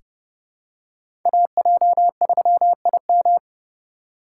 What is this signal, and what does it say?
Morse code, heard as a keyed tone, about 30 wpm, 700 Hz. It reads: AJ3IM
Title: USA Call Signs - 5 Characters - 4x Character Spacing - 30wpm